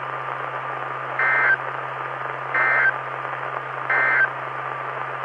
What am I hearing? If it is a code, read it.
Emergency Alert System (SAME) tones, heard as harsh buzzing bursts. It is heard from 1.1 to 4.3 s.
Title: Saline, AK SKYWARN Repeater System - W5RHS